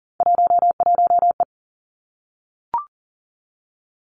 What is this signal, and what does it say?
Morse code, heard as a keyed tone, about 42 wpm, 700 Hz. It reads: 11E
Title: Three Challenging Letter-Number Combinations 40wpm